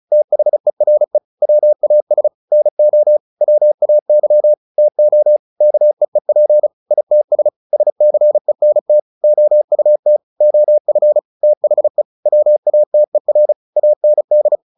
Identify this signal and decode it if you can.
Morse code, heard as a keyed tone, about 34 wpm, 600 Hz. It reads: THERE WAS NO WAY TO KEEP ITS SCENT OUT OF THE WATER AND
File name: the old man and the sea 2 35wpm